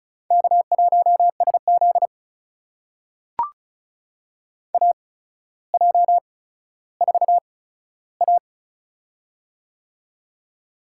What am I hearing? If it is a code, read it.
Morse code, heard as a keyed tone, about 36 wpm, spaced as for 15 wpm, 700 Hz. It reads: K1SZ AJ4A
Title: USA Call Signs - 4 Characters - 8x Character Spacing - 35wpm